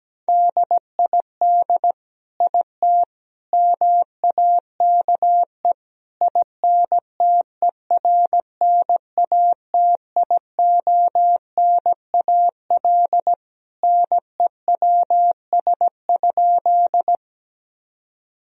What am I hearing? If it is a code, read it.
Morse code, heard as a keyed tone, about 17 wpm, 700 Hz. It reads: DID IT MAKE INTERNATIONAL NEWS?
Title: Sentences from Top 1000 Words - Review Part 4 - 17wpm